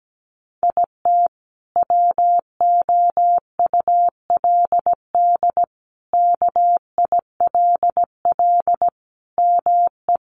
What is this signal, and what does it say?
Morse code, heard as a keyed tone, about 17 wpm, 700 Hz. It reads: IT WOULD KILL ME
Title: Sentences from Top 1400 Words 17wpm